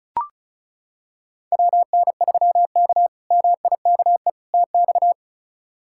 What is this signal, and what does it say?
Morse code, heard as a keyed tone, about 34 wpm, 700 Hz. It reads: WN3K MIKE TX
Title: K1USN SST QSO Practice - 35wpm